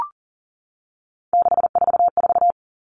Morse code, heard as a keyed tone, about 40 wpm, 700 Hz. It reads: 644
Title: Numbers - 3-Digit 40wpm